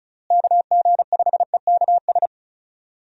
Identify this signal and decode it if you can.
Morse code, heard as a keyed tone, about 34 wpm, 700 Hz. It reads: KG5EKS
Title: USA Call Signs - 6 Characters - Repeated 2x - 35wpm